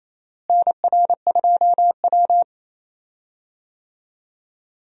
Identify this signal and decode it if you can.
Morse code, heard as a keyed tone, about 28 wpm, 700 Hz. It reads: NR2W
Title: USA Call Signs - 4 Characters - Repeated 3x - 28wpm